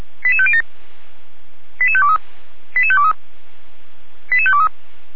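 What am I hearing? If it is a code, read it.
Five-tone selective call (ZVEI), heard as quick runs of five tones, tones of about 70 ms. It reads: 88507 88523 88523 88523
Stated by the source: ZVEI five-tone call 88507 received on SDR.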